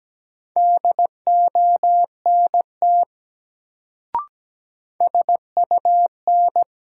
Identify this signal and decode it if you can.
Morse code, heard as a keyed tone, about 17 wpm, 700 Hz. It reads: DONT SUN
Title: Sentences from Top 300 Words - New Word Drill - 17wpm